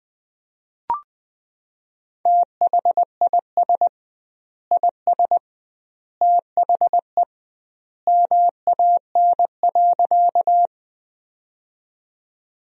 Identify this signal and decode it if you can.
Morse code, heard as a keyed tone, about 21 wpm, 700 Hz. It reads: THIS IS THE MAN.
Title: 2x Word-Spacing - Sentences from Top 200 Words - 20wpm